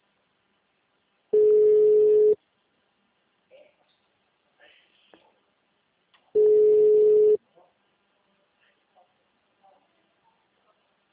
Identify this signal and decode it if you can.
Ringback tone (Europe, 425 Hz), with speech.